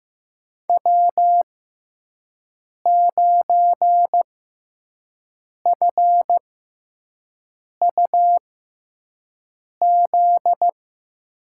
Morse code, heard as a keyed tone, about 15 wpm, spaced as for 5 wpm, 700 Hz. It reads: W9FUZ
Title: USA Call Signs - 5 Characters - 6x Character Spacing - 15wpm